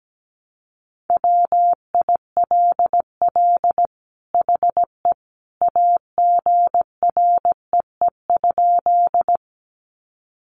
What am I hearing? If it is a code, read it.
Morse code, heard as a keyed tone, about 17 wpm, 700 Hz. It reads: WILL HE AGREE?
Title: Sentences from Top 1100 Words 17wpm